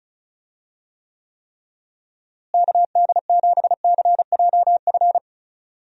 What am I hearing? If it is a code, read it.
Morse code, heard as a keyed tone, about 36 wpm, 700 Hz. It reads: KD7CJF
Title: USA Call Signs - 6 Characters - Repeated 3x - 35wpm